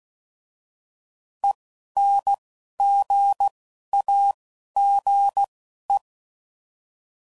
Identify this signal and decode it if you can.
Morse code, heard as a keyed tone, about 16 wpm, 800 Hz. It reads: E N G A G E